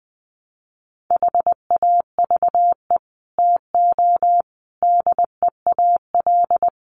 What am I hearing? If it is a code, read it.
Morse code, heard as a keyed tone, about 20 wpm, 700 Hz. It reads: HAVE TO DEAL
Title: Sets of 5 Words Encore2 20wpm